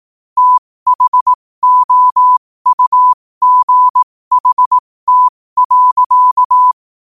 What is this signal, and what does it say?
Morse code, heard as a keyed tone, about 18 wpm, spaced as for 15 wpm, 1000 Hz. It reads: THOUGHT.
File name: qotdfiles / 15wpm